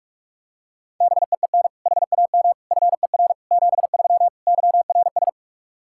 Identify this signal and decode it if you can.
Morse code, heard as a keyed tone, about 46 wpm, 700 Hz. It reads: BEEN HAM FER 73 YRS
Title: Practice QSO - Exchange 3 - More Info - 45wpm